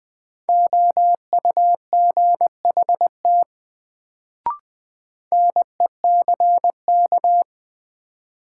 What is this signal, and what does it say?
Morse code, heard as a keyed tone, about 20 wpm, 700 Hz. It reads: OUGHT NECK
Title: Sets of 5 Words Encore - New Word Drill - 20wpm